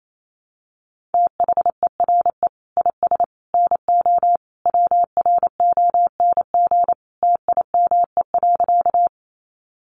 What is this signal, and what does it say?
Morse code, heard as a keyed tone, about 28 wpm, 700 Hz. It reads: THERE IS NO WRONG TIME.
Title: Sentences from Top 900 Words 28wpm